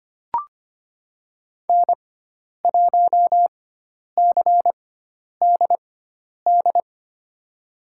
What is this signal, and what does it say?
Morse code, heard as a keyed tone, about 25 wpm, spaced as for 10 wpm, 700 Hz. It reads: N1CDD
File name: Lesson 16g - Letter c - Call Signs 25wpm